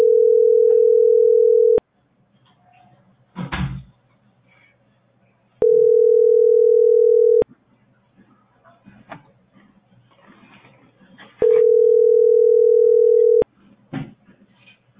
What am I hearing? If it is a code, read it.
Ringback tone (North America), 440 + 480 Hz.